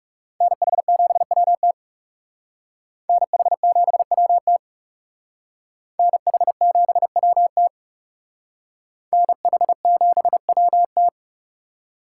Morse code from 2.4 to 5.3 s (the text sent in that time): NH7WT